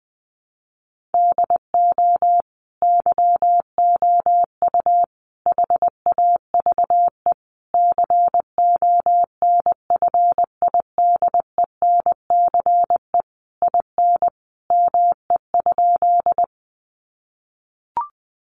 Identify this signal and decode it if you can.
Morse code, heard as a keyed tone, about 20 wpm, 700 Hz. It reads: DO YOU HAVE CONFIDENCE IN ME?
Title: Sentences from Top 1500 Words 20wpm